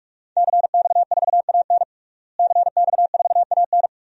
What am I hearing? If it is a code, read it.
Morse code, heard as a keyed tone, about 46 wpm, 700 Hz. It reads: CX4AN CX4AN
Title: DXCC-100 Call Signs - 5 Characters - Repeated 2x - 45wpm